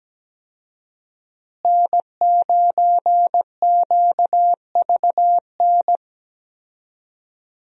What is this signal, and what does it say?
Morse code, heard as a keyed tone, about 17 wpm, 700 Hz. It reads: N9QVN
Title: USA Call Signs - 5 Characters - Repeated 2x - 17wpm